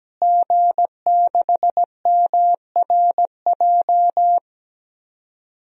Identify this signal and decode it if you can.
Morse code, heard as a keyed tone, about 17 wpm, 700 Hz. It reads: G6MRJ
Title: DXCC-100 Call Signs - 5 Characters - Repeated 3x - 17wpm